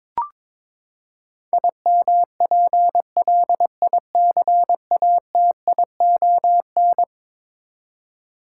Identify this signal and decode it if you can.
Morse code, heard as a keyed tone, about 22 wpm, 700 Hz. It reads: IMPLICATION
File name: practice / 22wpm / Eleven Letter Words - 22wpm